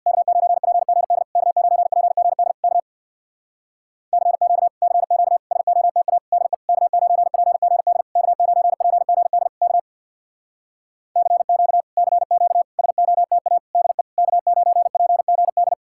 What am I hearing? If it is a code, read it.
Morse code, heard as a keyed tone, about 65 wpm, 700 Hz. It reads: K9PGD K9PGD K CQ CQ SOTA DE K9PGD K9PGD K CQ CQ SOTA DE K9PGD